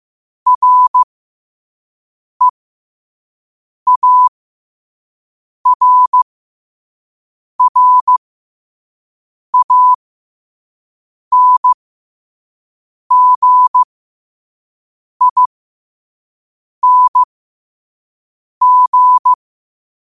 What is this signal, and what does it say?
Morse code, heard as a keyed tone, about 15 wpm, spaced as for 5 wpm, 1000 Hz. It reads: REARRANGING
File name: qotdfiles / 5wpm